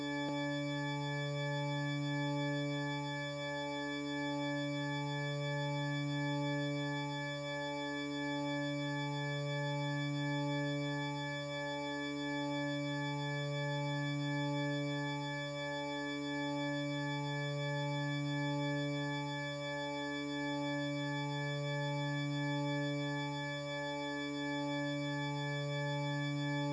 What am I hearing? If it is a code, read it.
There is no code.